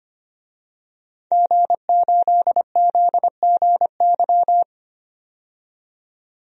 Morse code, heard as a keyed tone, about 25 wpm, 700 Hz. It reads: G8ZGY
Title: DXCC-100 Call Signs - 5 Characters - 25wpm